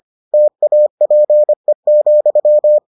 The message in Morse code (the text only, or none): TAPE,